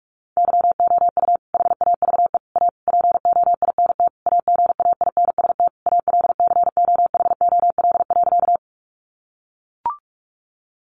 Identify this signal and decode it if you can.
Morse code, heard as a keyed tone, about 44 wpm, 700 Hz. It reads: YOU HAVE A POINT AGAINST ALCOHOL.
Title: Sentences from Top 1400 Words 45wpm